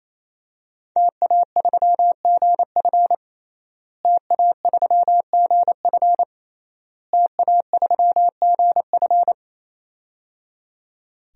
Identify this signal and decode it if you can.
Morse code, heard as a keyed tone, about 27 wpm, 700 Hz. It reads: TA3GF TA3GF TA3GF
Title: DXCC-100 Call Signs - 5 Characters - Repeated 3x - 28wpm